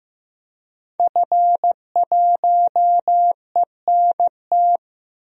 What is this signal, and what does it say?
Morse code, heard as a keyed tone, about 15 wpm, 700 Hz. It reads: F1ENT
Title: DXCC-100 Call Signs - 5 Characters - Repeated 2x - 15wpm